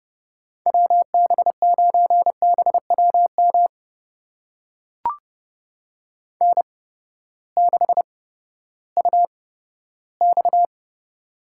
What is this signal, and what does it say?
Morse code, heard as a keyed tone, about 31 wpm, spaced as for 15 wpm, 700 Hz. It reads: WB9BWM N6UX
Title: USA Call Signs - 8x Character Spacing - 30wpm